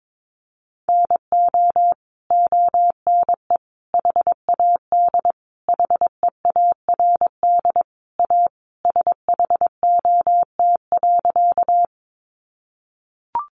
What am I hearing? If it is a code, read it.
Morse code, heard as a keyed tone, about 22 wpm, 700 Hz. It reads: NO ONE HAD HEARD A SHOT.
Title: Sentences from Top 1100 Words 22wpm